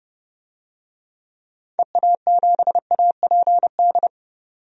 Morse code, heard as a keyed tone, about 31 wpm, 700 Hz. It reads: EA7APD
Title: DXCC-100 Call Signs - 6 Characters - 30wpm